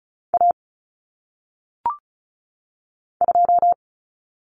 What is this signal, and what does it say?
Morse code, heard as a keyed tone, about 36 wpm, 700 Hz. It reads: A 2
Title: Common Cut Numbers - 1-Digit 35wpm